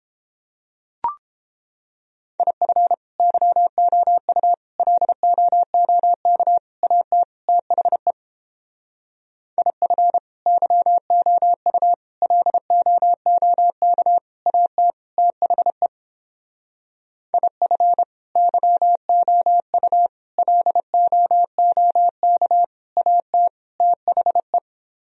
Morse code, heard as a keyed tone, about 29 wpm, 700 Hz. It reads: IF YOU LOOK AT THE IF YOU LOOK AT THE IF YOU LOOK AT THE